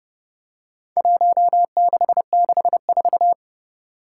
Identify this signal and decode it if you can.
Morse code, heard as a keyed tone, about 30 wpm, 700 Hz. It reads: 1664
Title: Four Challenging Letter-Number Combinations 30wpm